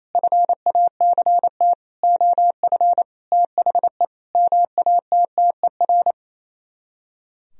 Morse code, heard as a keyed tone, about 27 wpm, 700 Hz. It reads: FACT OF THE MATTER
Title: Speed-Racing - Sets of 5 Words 28wpm